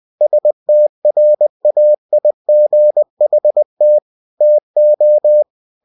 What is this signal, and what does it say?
Morse code, heard as a keyed tone, about 20 wpm, 600 Hz. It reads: STRAIGHT TO
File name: cw-books / winnie-the-pooh / winnie-20wpm-page-0040